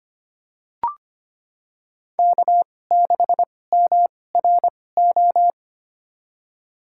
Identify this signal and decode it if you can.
Morse code, heard as a keyed tone, about 25 wpm, spaced as for 18 wpm, 700 Hz. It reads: K6MRO
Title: USA Call Signs - 5 Characters - 2x Character Spacing - 25wpm